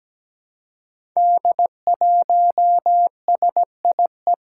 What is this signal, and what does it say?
Morse code, heard as a keyed tone, about 17 wpm, 700 Hz. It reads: D1SIE
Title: Five Challenging Letter-Number Combinations 17wpm